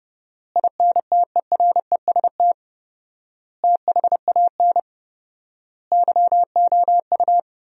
Morse code, heard as a keyed tone, about 31 wpm, 700 Hz. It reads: INTEREST THAN YOU
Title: 4x Word-Spacing - Sentences from Top 400 Words - 30wpm